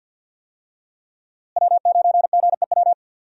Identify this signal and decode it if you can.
Morse code, heard as a keyed tone, about 49 wpm, 700 Hz. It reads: W9GEW
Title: USA Call Signs - 5 Characters - Repeated 2x - 50wpm